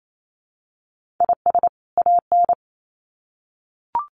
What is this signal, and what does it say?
Morse code, heard as a keyed tone, about 28 wpm, 700 Hz. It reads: IS AN
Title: Sets of 2 Words 28wpm